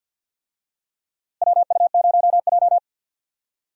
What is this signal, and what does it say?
Morse code, heard as a keyed tone, about 48 wpm, 700 Hz. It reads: WU0J
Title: USA Call Signs - 4 Characters - Repeated 3x - 50wpm